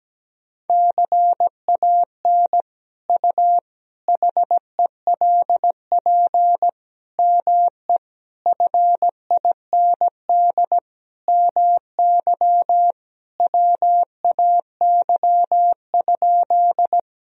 Morse code, heard as a keyed tone, about 17 wpm, 700 Hz. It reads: CAN U HELP ME FIND MY WAY?
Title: Sentences from Top 200 Words and Abbreviations 17wpm